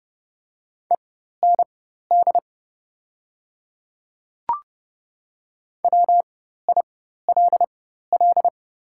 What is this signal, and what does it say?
Morse code, heard as a keyed tone, about 30 wpm, spaced as for 14 wpm, 700 Hz. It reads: END WILL
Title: 4x Character Spacing - Top 200 Words - 30wpm